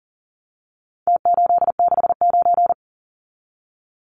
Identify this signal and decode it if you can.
Morse code, heard as a keyed tone, about 40 wpm, 700 Hz. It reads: T869
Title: Numbers - Contest Style - 4-Digit 40wpm